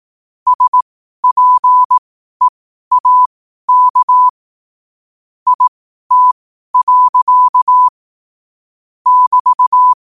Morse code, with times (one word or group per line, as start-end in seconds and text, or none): 0.5-4.3 SPEAK
5.5-7.9 IT.
9.1-9.9 <BT>